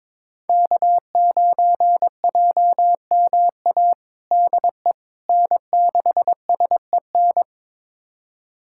Morse code, heard as a keyed tone, about 22 wpm, 700 Hz. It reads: K9JMA DE N6SEN
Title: USA Callsign DE Callsign - 5 Characters - 22wpm